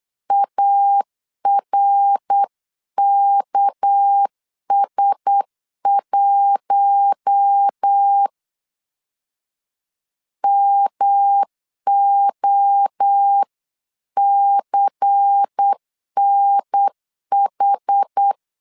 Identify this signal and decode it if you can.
Morse code, heard as a keyed tone, about 8 wpm, 800 Hz. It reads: ARKS1 MOCNH
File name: CW / DET / 1 6 1500C S B 8wpm 5142